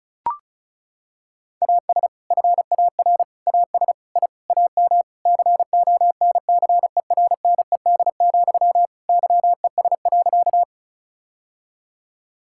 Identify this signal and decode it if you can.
Morse code, heard as a keyed tone, about 34 wpm, 700 Hz. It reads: AS FAR AS I AM CONCERNED, YES.
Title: Sentences from Top 1000 Words - Review Part 6 - 35wpm